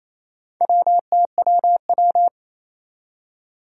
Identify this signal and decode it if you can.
Morse code, heard as a keyed tone, about 28 wpm, 700 Hz. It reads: WTWW
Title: Four Challenging Letter-Number Combinations 28wpm